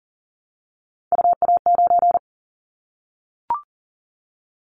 Morse code, heard as a keyed tone, about 40 wpm, 700 Hz. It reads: UA9
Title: Callsign Prefixes for Top 50 DXCC - 40wpm